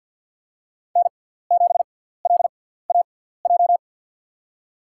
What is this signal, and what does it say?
Morse code, heard as a keyed tone, about 48 wpm, spaced as for 17 wpm, 700 Hz. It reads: N7LAJ
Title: USA Call Signs - 5 Characters - 6x Character Spacing - 50wpm